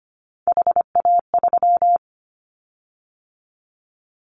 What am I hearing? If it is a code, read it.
Morse code, heard as a keyed tone, about 25 wpm, 700 Hz. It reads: HA3
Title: Callsign Prefixes for Top 50 DXCC - 25wpm